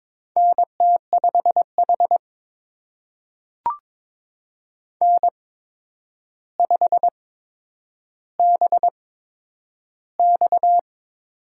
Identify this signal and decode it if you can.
Morse code, heard as a keyed tone, about 22 wpm, spaced as for 9 wpm, 700 Hz. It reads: NT5H N5BX
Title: USA Call Signs - 4 Characters - 8x Character Spacing - 22wpm